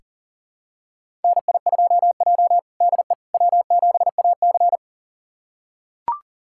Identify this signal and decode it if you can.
Morse code, heard as a keyed tone, about 40 wpm, 700 Hz. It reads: NI2J DE W7AC
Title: USA Callsign DE Callsign - 4 Characters - 40wpm